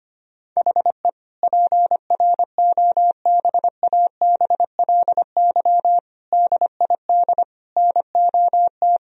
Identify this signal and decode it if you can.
Morse code, heard as a keyed tone, about 25 wpm, 700 Hz. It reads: HE PROBABLY DID NOT
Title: Sentences from Top 1000 Words - Review Part 4 - 25wpm